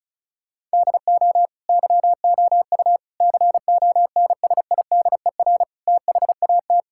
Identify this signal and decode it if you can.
Morse code, heard as a keyed tone, about 34 wpm, 700 Hz. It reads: DO YOU CONSIDER THAT
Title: Sentences from Top 1000 Words - Review Part 6 - 35wpm